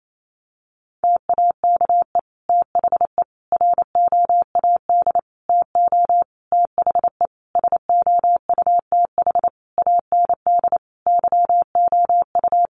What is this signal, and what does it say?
Morse code, heard as a keyed tone, about 28 wpm, 700 Hz. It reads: TAKE THE ROAD TO THE SOUTH AND YOU
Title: Sentences from Top 400 Words 28wpm